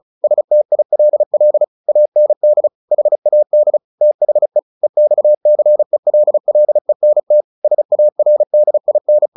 Morse code, heard as a keyed tone, about 37 wpm, 600 Hz. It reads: STILL AND HAD THE EXCELLENT SARDIN
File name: the old man and the sea / the old man and the sea 1 35wpm